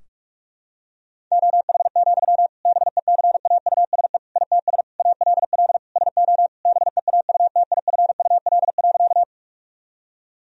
Morse code, heard as a keyed tone, about 44 wpm, 700 Hz. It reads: OH, BECAUSE ITS ALL SO BEAUTIFUL.
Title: Sentences from Top 1000 Words - Review Part 3 - 45wpm